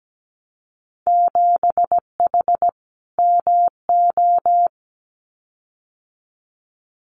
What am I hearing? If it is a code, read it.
Morse code, heard as a keyed tone, about 17 wpm, 700 Hz. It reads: 7H MO
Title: Winter Field Day Exchanges Encore 17wpm